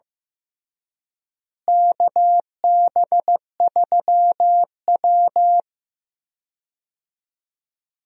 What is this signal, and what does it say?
Morse code, heard as a keyed tone, about 15 wpm, 700 Hz. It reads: KB3W
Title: USA Call Signs - 4 Characters - Repeated 2x - 15wpm